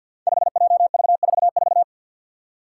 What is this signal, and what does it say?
Morse code, heard as a keyed tone, about 51 wpm, 700 Hz. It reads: 5JV44
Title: Five Challenging Letter-Number Combinations 50wpm